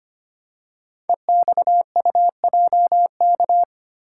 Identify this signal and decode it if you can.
Morse code, heard as a keyed tone, about 25 wpm, 700 Hz. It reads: EXUJK
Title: Code Groups - 5 Letters - 25wpm